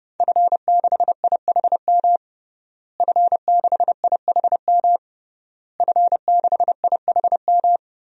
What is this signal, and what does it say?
Morse code, heard as a keyed tone, about 30 wpm, 700 Hz. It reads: F6IHM F6IHM F6IHM
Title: DXCC-100 Call Signs - 5 Characters - Repeated 3x - 30wpm